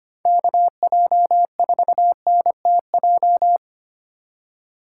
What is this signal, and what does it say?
Morse code, heard as a keyed tone, about 25 wpm, 700 Hz. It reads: KJ4NTJ
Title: USA Call Signs - 6 Characters - 3x Character Spacing - 25wpm